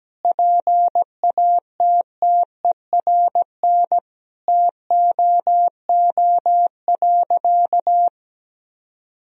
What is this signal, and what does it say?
Morse code, heard as a keyed tone, about 17 wpm, 700 Hz. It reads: PATTERN TOO.
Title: Sentences from Top 1000 Words - Review Part 2 - 17wpm